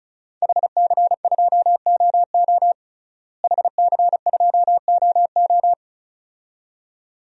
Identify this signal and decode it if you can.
Morse code, heard as a keyed tone, about 36 wpm, 700 Hz. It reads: HC2OO HC2OO
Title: DXCC-100 Call Signs - 5 Characters - Repeated 2x - 35wpm